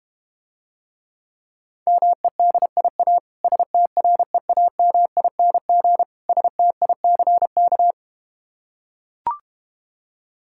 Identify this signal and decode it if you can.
Morse code, heard as a keyed tone, about 32 wpm, 700 Hz. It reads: MEDIA STREAMING STICK